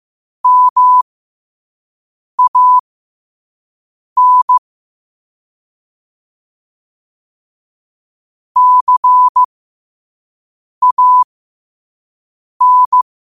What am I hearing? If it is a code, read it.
Morse code, heard as a keyed tone, about 15 wpm, spaced as for 5 wpm, 1000 Hz. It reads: MAN CAN
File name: qotdfiles / 5wpm